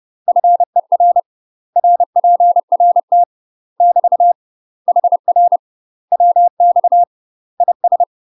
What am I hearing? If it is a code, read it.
Morse code, heard as a keyed tone, about 30 wpm, 700 Hz. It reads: FER RPRT <BT> HR WX IS
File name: UK MASTER QSOs TRAINING / 30WPM / UK MASTER QSOs 0008